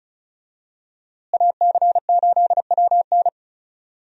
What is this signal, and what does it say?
Morse code, heard as a keyed tone, about 36 wpm, 700 Hz. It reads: AC8WN
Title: USA Call Signs - 5 Characters - Repeated 2x - 35wpm